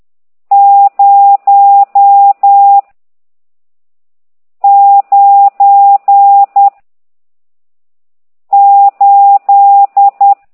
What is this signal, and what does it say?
Morse code, heard as a keyed tone, about 10 wpm, 800 Hz. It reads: 0 9 8
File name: ct2idw / exerc14